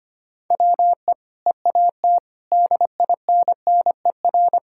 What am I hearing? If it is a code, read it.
Morse code, heard as a keyed tone, about 25 wpm, 690 Hz. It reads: WE EAT DINNER
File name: Easy Sentences with Hints - 25wpm